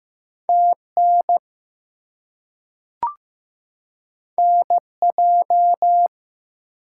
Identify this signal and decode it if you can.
Morse code, heard as a keyed tone, about 15 wpm, 700 Hz. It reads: TN NJ
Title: US States and Canadian Provinces 15wpm